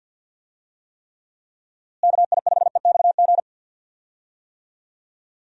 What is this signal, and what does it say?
Morse code, heard as a keyed tone, about 52 wpm, 700 Hz. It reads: KI5EXG
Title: USA Call Signs - 6 Characters - Repeated 2x - 50wpm